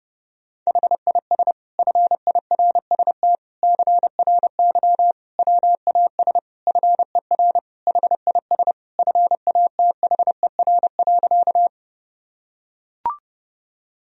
Morse code, heard as a keyed tone, about 29 wpm, 700 Hz. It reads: HIS FIRST CRY WAS FER HIS FATHER.
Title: Sentences from Top 500 Words and Abbreviations 30wpm